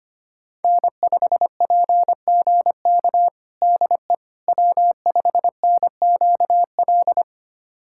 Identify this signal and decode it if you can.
Morse code, heard as a keyed tone, about 25 wpm, 700 Hz. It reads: N5PGK DE W5NQL
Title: USA Callsign DE Callsign - 5 Characters - 25wpm